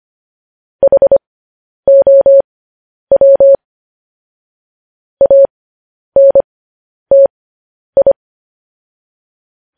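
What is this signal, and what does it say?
Morse code, heard as a keyed tone, about 25 wpm, spaced as for 10 wpm, 550 Hz. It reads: HOW ANTI
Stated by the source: Morse code news headlines broadcast at 10 wpm.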